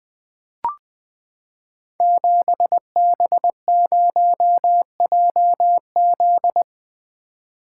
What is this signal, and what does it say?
Morse code, heard as a keyed tone, about 20 wpm, 700 Hz. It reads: 7B0JZ
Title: Code Groups - 5 Letter:Numbers - 20wpm